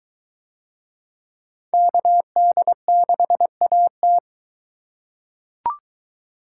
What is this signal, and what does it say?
Morse code, heard as a keyed tone, about 22 wpm, 700 Hz. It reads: KD6AT